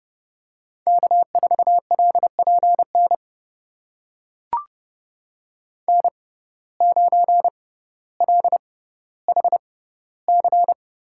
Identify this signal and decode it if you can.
Morse code, heard as a keyed tone, about 30 wpm, spaced as for 15 wpm, 700 Hz. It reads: K4LPN N9LHC
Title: USA Call Signs - 5 Characters - 6x Character Spacing - 30wpm